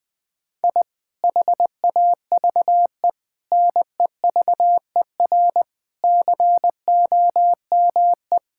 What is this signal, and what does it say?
Morse code, heard as a keyed tone, about 20 wpm, 700 Hz. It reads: I HAVE NEVER COME
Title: Sentences from Top 1000 Words - Review Part 5 - 20wpm